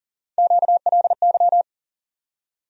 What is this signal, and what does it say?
Morse code, heard as a keyed tone, about 40 wpm, 700 Hz. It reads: QLY
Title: USA Call Sign Postfixes - 3 Letters 40wpm